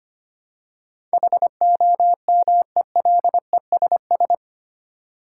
Morse code, heard as a keyed tone, about 25 wpm, 700 Hz. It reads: HOMELESS
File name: 8 Letter Words - First 3 Letters Given - 25wpm